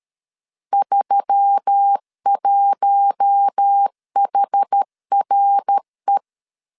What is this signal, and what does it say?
Morse code, heard as a keyed tone, about 13 wpm, 800 Hz. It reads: 31HRE
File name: CW / DET / 1 6 1500C S B 12wpm 3431